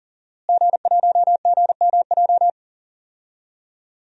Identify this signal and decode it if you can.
Morse code, heard as a keyed tone, about 40 wpm, 700 Hz. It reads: G1GMJ
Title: DXCC-100 Call Signs - 5 Characters - Repeated 2x - 40wpm